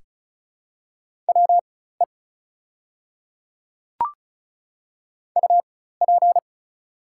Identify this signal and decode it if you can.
Morse code, heard as a keyed tone, about 35 wpm, spaced as for 16 wpm, 690 Hz. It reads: WE UP
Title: 4x Character Spacing - Top 100 Words - 35wpm